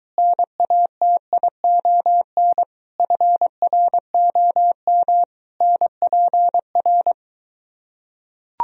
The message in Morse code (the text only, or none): NATION FROM NPR